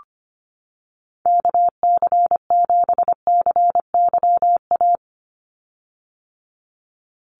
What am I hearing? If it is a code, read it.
Morse code, heard as a keyed tone, about 25 wpm, 700 Hz. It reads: KC7CYA
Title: USA Call Signs 25wpm